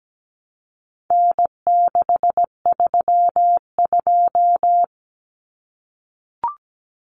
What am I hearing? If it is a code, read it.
Morse code, heard as a keyed tone, about 17 wpm, 700 Hz. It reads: N632
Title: Common Cut Numbers - 4-Digit 17wpm